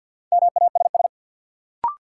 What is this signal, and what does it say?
Morse code, heard as a keyed tone, about 46 wpm, 700 Hz. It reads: MASS
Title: Sets of 5 Words Encore - New Word Drill - 50wpm